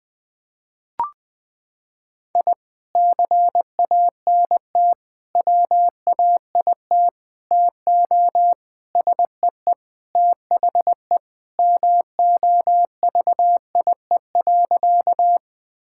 Morse code, heard as a keyed tone, about 21 wpm, 700 Hz. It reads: I CANT WAIT TO SEE THE MOVIE.
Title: Sentences with 3-Word Phrases - 20wpm